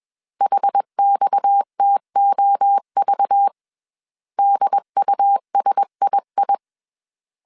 Morse code, heard as a keyed tone, about 21 wpm, 800 Hz. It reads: HXTOV DUSII
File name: CW / DET / 1 6 1500C S C 20wpm 2116